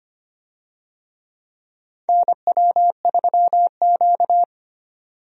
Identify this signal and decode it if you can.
Morse code, heard as a keyed tone, about 25 wpm, 700 Hz. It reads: NW3Q
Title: USA Call Signs - 4 Characters - 25wpm